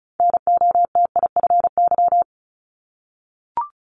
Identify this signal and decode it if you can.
Morse code, heard as a keyed tone, about 36 wpm, 690 Hz. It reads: NOTIFY
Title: Multi-letter Phonemes 35wpm